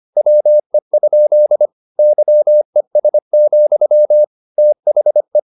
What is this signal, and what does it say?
Morse code, heard as a keyed tone, about 25 wpm, 600 Hz. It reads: WE? YES, THE